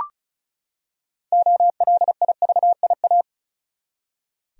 Morse code, heard as a keyed tone, about 34 wpm, 700 Hz. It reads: OLIVIA
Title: USA Common Names - 35wpm